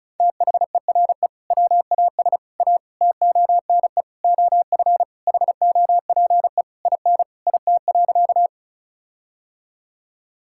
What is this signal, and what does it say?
Morse code, heard as a keyed tone, about 34 wpm, 700 Hz. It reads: THERE WAS A TONE OF HOPE IN IT.
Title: Sentences from Top 1600 Words 35wpm